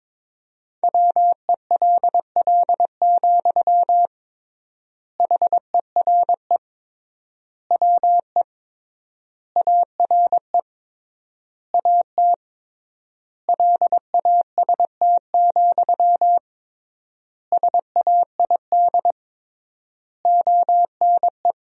Morse code, heard as a keyed tone, about 22 wpm, 700 Hz. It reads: WELL, HERE WE ARE AT LAST, SAID ONE
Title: 3x Word-Spacing - Sentences from Top 300 Words - 22wpm